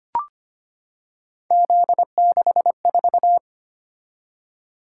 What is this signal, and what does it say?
Morse code, heard as a keyed tone, about 25 wpm, 700 Hz. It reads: Z64
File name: Three Letter-Number Combinations - 25wpm